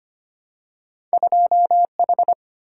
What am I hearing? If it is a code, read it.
Morse code, heard as a keyed tone, about 25 wpm, 700 Hz. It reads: 2H
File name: Two Letter-Number Combinations - 25wpm